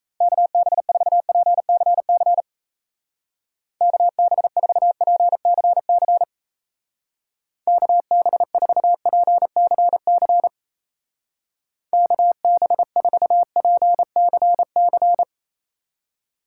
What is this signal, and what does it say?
Morse code, heard as a keyed tone, about 36 wpm, 700 Hz. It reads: KB4PCC KB4PCC KB4PCC KB4PCC